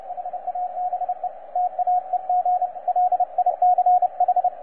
Morse code, heard as a keyed tone, about 30 wpm, 720 Hz, with radio noise. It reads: HABE TAEG LICH
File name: posts / xw2cw-10-cw